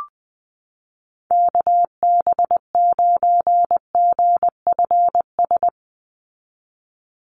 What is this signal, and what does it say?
Morse code, heard as a keyed tone, about 20 wpm, 700 Hz. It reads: KB9GFS
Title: USA Call Signs 20wpm